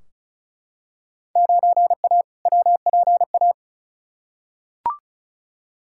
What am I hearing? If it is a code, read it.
Morse code, heard as a keyed tone, about 34 wpm, 700 Hz. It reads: 9A WPA
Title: Field Day Exchanges Encore - 35wpm